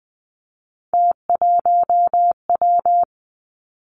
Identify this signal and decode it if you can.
Morse code, heard as a keyed tone, about 20 wpm, 700 Hz. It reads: T1W
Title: Three Challenging Letter-Number Combinations 20wpm